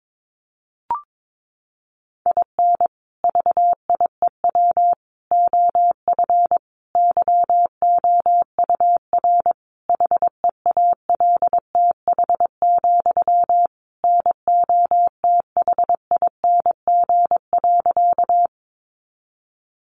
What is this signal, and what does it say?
Morse code, heard as a keyed tone, about 22 wpm, 700 Hz. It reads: IN VIEW OF YOUR HEALTH, NOTHING.
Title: Sentences from Top 900 Words 22wpm